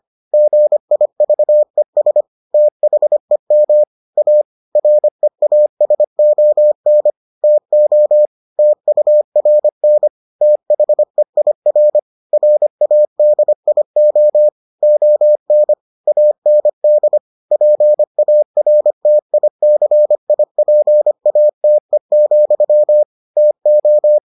Morse code, heard as a keyed tone, about 25 wpm, 600 Hz. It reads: GIVES THEM A REASON TO TURN THEIR RADIO ON AND PARTICIPATE, TO